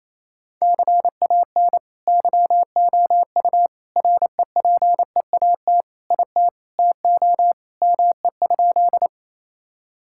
Morse code, heard as a keyed tone, about 28 wpm, 700 Hz. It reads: CAN YOU REPEAT IT TO ME?
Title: Sentences from Top 1600 Words 28wpm